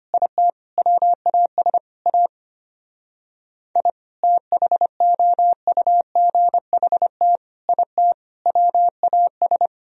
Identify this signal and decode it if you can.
Morse code, heard as a keyed tone, about 26 wpm, 700 Hz. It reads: IT WAS A I THOUGHT IT WAS
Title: Speed-Racing - Sets of 5 Words 25wpm